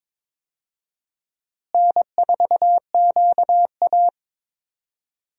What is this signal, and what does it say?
Morse code, heard as a keyed tone, about 21 wpm, 700 Hz. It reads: N4QA
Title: USA Call Signs - 4 Characters - Repeated 3x - 22wpm